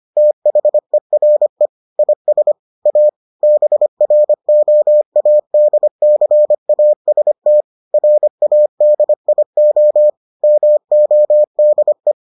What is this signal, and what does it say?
Morse code, heard as a keyed tone, about 25 wpm, 600 Hz. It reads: THERE IS A BROADCAST RADIO MODE